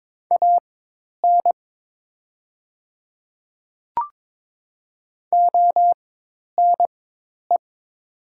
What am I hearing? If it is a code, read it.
Morse code, heard as a keyed tone, about 23 wpm, spaced as for 10 wpm, 700 Hz. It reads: AN ONE
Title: 4x Character Spacing - Top 100 Words - 22wpm